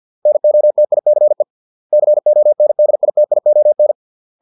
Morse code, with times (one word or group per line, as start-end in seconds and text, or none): 0.3-1.4 NOTICE
1.9-3.9 CONDITION